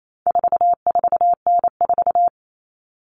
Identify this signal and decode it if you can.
Morse code, heard as a keyed tone, about 28 wpm, 700 Hz. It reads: 44N4
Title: All Cut Numbers - 4-Digit 28wpm